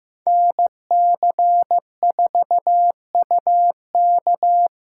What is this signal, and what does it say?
Morse code, heard as a keyed tone, about 18 wpm, 700 Hz. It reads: NC4UK
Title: Speed Racing - USA 2x2 Call Signs 15wpm